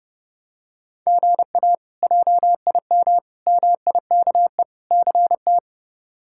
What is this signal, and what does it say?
Morse code, heard as a keyed tone, about 30 wpm, 700 Hz. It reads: GA JIM MIKE CT
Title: K1USN SST QSO Practice - 30wpm